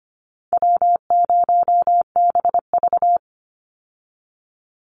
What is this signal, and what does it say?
Morse code, heard as a keyed tone, about 25 wpm, 700 Hz. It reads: W0BV
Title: Easy Call Signs Repeated 2x - 25wpm